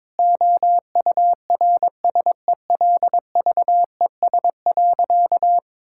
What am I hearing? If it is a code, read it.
Morse code, heard as a keyed tone, about 22 wpm, 700 Hz. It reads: OURSELVES.
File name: Difficult Sentences with Hints - 22wpm